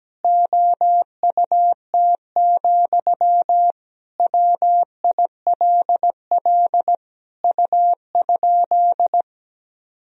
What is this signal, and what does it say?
Morse code, heard as a keyed tone, about 17 wpm, 700 Hz. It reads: OUT, WILL U?
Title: Sentences from Top 300 Words and Abbreviations 17wpm